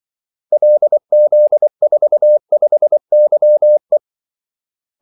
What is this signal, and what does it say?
Morse code, heard as a keyed tone, about 24 wpm, 600 Hz. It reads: LZ45YE
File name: Callsigns-24wpm